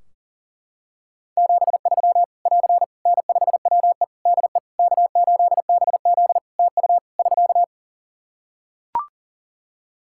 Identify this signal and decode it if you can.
Morse code, heard as a keyed tone, about 40 wpm, 700 Hz. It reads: 73 <AR> N5WE DE K8BZ TU <SK>
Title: K1USN SST QSO Practice 40wpm